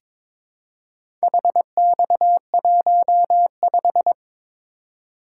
Morse code, heard as a keyed tone, about 22 wpm, 700 Hz. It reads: HX15
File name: practice / 22wpm / Four Letter-Number Combinations - 22wpm